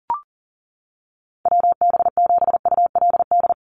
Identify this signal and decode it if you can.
Morse code, heard as a keyed tone, about 41 wpm, 700 Hz. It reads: WB7ULD
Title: NA SOTA Chaser Callsigns 40wpm